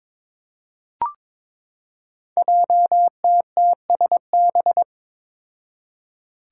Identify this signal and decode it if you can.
Morse code, heard as a keyed tone, about 22 wpm, 700 Hz. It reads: JTTSB
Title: Five Challenging Letter-Number Combinations 22wpm